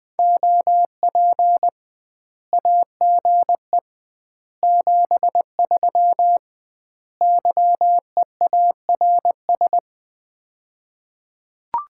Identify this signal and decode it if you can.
Morse code, heard as a keyed tone, about 20 wpm, 700 Hz. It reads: OP AGE 73 YEARS
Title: Common QSO Phrases - 2x Word-Spacing - 20wpm